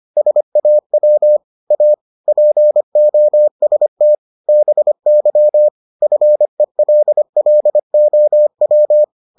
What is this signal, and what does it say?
Morse code, heard as a keyed tone, about 25 wpm, 600 Hz. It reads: SAW A POST BY FELLOW